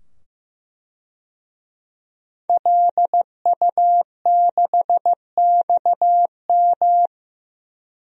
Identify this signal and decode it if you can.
Morse code, heard as a keyed tone, about 15 wpm, 700 Hz. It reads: LU6XM
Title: DXCC-100 Call Signs - 5 Characters - 15wpm